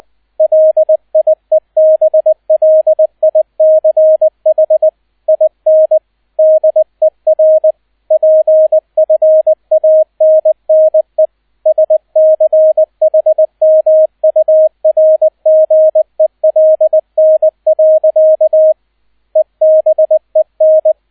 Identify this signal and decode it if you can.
Morse code, heard as a keyed tone, about 20 wpm, 630 Hz, with radio noise. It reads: LIEBLICH IN DER PFANNE SCHMURGELN. EBEN
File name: 002 Zweiter Streich 20wpm